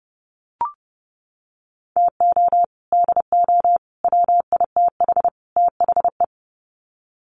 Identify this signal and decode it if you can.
Morse code, heard as a keyed tone, about 30 wpm, 700 Hz. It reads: TO DO WITH THE
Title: Sets of 4 Words 30wpm